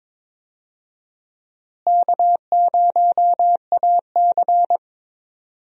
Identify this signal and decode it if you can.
Morse code, heard as a keyed tone, about 22 wpm, 700 Hz. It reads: K0AC
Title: USA Call Signs - 4 Characters - Repeated 2x - 22wpm